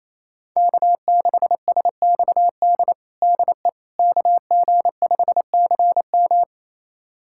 Morse code, heard as a keyed tone, about 28 wpm, 700 Hz. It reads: K6SXD DE KG5CM
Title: USA Callsign DE Callsign - 5 Characters - 28wpm